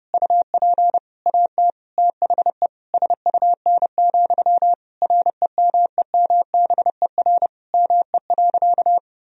Morse code, heard as a keyed tone, about 30 wpm, 700 Hz. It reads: UP AT THE SUN, REMEMBER ME.
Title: Sentences from Top 1000 Words - Review Part 1 - 30wpm